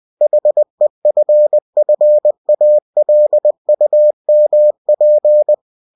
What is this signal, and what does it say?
Morse code, heard as a keyed tone, about 20 wpm, 600 Hz. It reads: HEFFALUMP
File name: winnie-20wpm